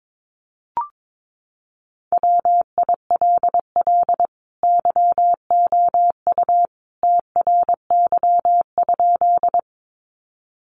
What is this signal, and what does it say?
Morse code, heard as a keyed tone, about 22 wpm, 700 Hz. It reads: WILL YOU TRY?
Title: Sentences from Top 200 Words 22wpm